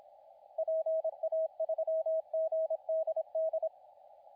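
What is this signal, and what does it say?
Morse code, heard as a keyed tone, about 26 wpm, 640 Hz, with radio noise. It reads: PA3GDD
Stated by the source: CW contest audio recorded at LA4C during the SAC CW contest.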